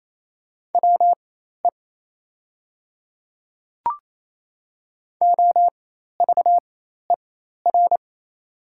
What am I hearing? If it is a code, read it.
Morse code, heard as a keyed tone, about 28 wpm, spaced as for 13 wpm, 700 Hz. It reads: WE OVER
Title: 4x Character Spacing - Top 200 Words - 28wpm